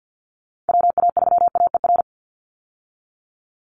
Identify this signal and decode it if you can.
Morse code, heard as a keyed tone, about 50 wpm, 700 Hz. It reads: WA3AER
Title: Call Signs Repeated 2x - 50wpm